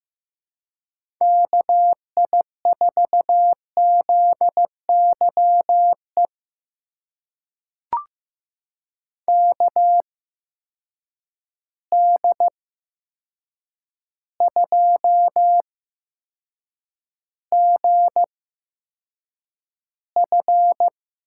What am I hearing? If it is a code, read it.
Morse code, heard as a keyed tone, about 15 wpm, spaced as for 7 wpm, 700 Hz. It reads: KI4ZYE KD2GF
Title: USA Call Signs - 8x Character Spacing - 15wpm